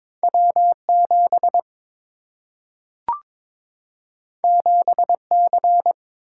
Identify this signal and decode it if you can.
Morse code, heard as a keyed tone, about 22 wpm, 700 Hz. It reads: W7 7C
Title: Single-Letter Single-Number 22wpm